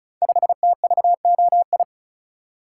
Morse code, heard as a keyed tone, about 36 wpm, 700 Hz. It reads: 5TVOI
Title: Code Groups - 5 Letter:Numbers - 35wpm